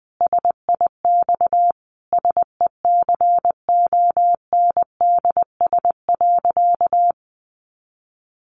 Morse code, heard as a keyed tone, about 20 wpm, 700 Hz. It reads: SIX SECONDS.
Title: A Sound Of Thunder 20wpm